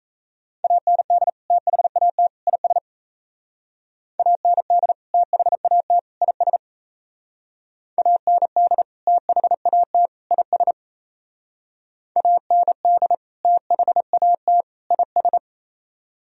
Morse code, heard as a keyed tone, about 37 wpm, 700 Hz. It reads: AND THAT IS AND THAT IS AND THAT IS AND THAT IS